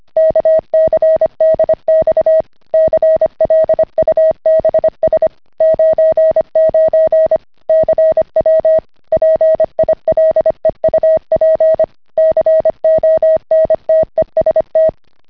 Morse code, heard as a keyed tone, about 25 wpm, 640 Hz. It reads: KCDX CLUBS 99 CW PILEUP CONTEST